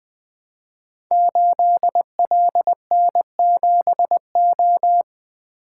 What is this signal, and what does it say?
Morse code, heard as a keyed tone, about 20 wpm, 700 Hz. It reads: 8LN7O
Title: Code Groups - 5 Letter:Numbers - 20wpm